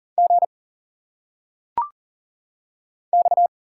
Morse code, heard as a keyed tone, about 40 wpm, 700 Hz. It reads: G X
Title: Single Letters - Phonetic Pronunciation 40wpm